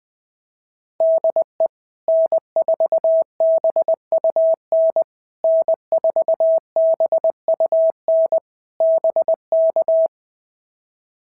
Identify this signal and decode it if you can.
Morse code, heard as a keyed tone, about 18 wpm, 650 Hz. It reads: DE N4BUN N4BUN BK
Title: POTA QSO Practice - 20wpm